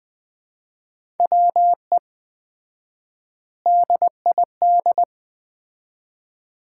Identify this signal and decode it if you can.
Morse code, heard as a keyed tone, about 20 wpm, 700 Hz. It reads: WE DID
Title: 4x Word-Spacing - Sets of 4 Words - 20wpm